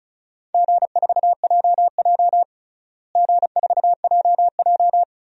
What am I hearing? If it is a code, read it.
Morse code, heard as a keyed tone, about 36 wpm, 700 Hz. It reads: G4JJ G4JJ
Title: DXCC-100 Call Signs - 4 Characters - Repeated 3x - 35wpm